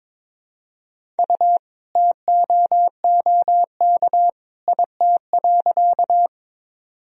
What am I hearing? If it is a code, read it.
Morse code, heard as a keyed tone, about 22 wpm, 700 Hz. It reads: U TOOK IT.
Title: Sentences from Top 300 Words and Abbreviations 22wpm